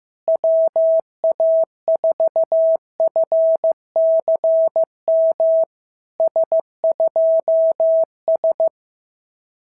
Morse code, heard as a keyed tone, about 14 wpm, 650 Hz. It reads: WA4FCM S2S
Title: SOTA QSO Practice 15wpm